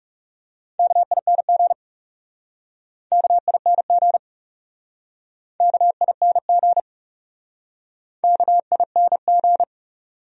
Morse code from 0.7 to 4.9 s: KING KING

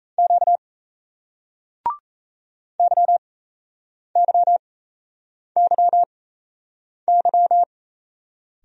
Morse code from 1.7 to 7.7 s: YYYY

